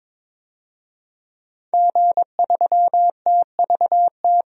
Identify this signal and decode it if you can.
Morse code, heard as a keyed tone, about 22 wpm, 700 Hz. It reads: G3TVT
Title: DXCC-100 Call Signs - 5 Characters - 22wpm